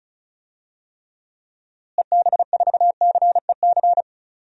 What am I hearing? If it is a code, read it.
Morse code, heard as a keyed tone, about 36 wpm, 700 Hz. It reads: EB4CEC
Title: DXCC-100 Call Signs - 6 Characters - Repeated 3x - 35wpm